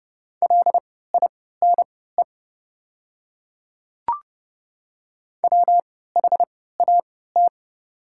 Morse code, heard as a keyed tone, about 29 wpm, spaced as for 17 wpm, 700 Hz. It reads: LINE WHAT
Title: 3x Character Spacing - Top 200 Words - 30wpm